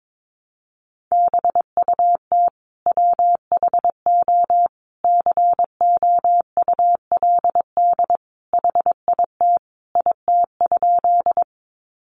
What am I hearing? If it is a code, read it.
Morse code, heard as a keyed tone, about 22 wpm, 700 Hz. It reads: BUT WHO COULD HIT IT?
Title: Sentences from Top 1200 Words 22wpm